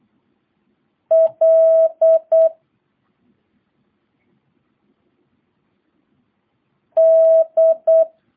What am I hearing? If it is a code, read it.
Morse code, heard as a keyed tone, about 8 wpm, 650 Hz. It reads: L D